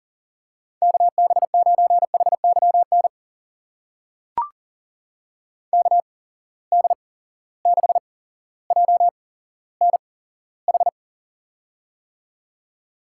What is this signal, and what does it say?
Morse code, heard as a keyed tone, about 39 wpm, spaced as for 17 wpm, 700 Hz. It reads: KB9HYN KD6JNH
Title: USA Call Signs - 6 Characters - 8x Character Spacing - 40wpm